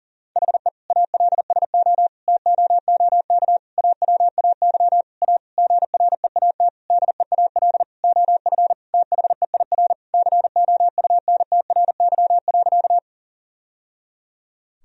Morse code, heard as a keyed tone, about 40 wpm, 700 Hz. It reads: HE ALSO TOOK AWAY A GREAT DEAL OF THEIR COUNTRY.
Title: Sentences from Top 1000 Words - Review Part 1 - 40wpm